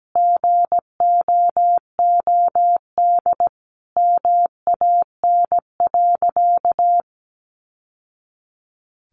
Morse code, heard as a keyed tone, about 17 wpm, 700 Hz. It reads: GOOD MAN.
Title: Sentences from Top 1200 Words 17wpm